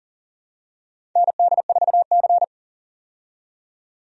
Morse code, heard as a keyed tone, about 38 wpm, 700 Hz. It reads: ND4C
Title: USA Call Signs - 4 Characters - 40wpm